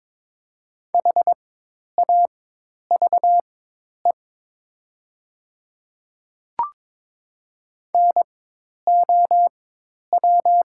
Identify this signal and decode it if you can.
Morse code, heard as a keyed tone, about 22 wpm, spaced as for 10 wpm, 700 Hz. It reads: HAVE NOW
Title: 4x Character Spacing - Top 100 Words - 22wpm